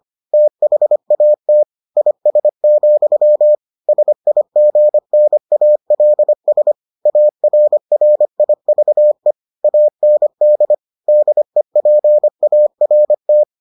Morse code, heard as a keyed tone, about 25 wpm, 600 Hz. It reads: THAT IS, SIGNALS ARRIVE AND DEPART